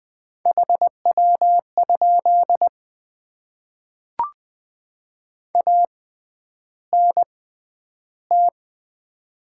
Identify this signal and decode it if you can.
Morse code, heard as a keyed tone, about 20 wpm, 700 Hz. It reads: H W ? A N T
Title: Top 20 QSO Elements - 6x Character Spacing - 20wpm